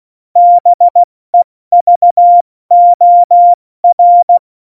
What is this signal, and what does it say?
Morse code, heard as a keyed tone, about 16 wpm, 700 Hz. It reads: BEVOR
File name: Morsekurs / Witze / Witz 20. November 2018, 70 F80